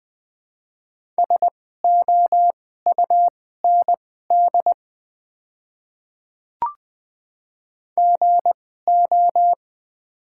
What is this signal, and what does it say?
Morse code, heard as a keyed tone, about 21 wpm, spaced as for 15 wpm, 700 Hz. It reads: SOUND GO
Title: 2x Character Spacing - Top 100 Words - 20wpm